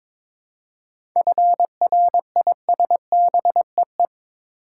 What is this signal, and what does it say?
Morse code, heard as a keyed tone, about 21 wpm, 690 Hz. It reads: FRISBEE
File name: Difficult Words with Hints - 22wpm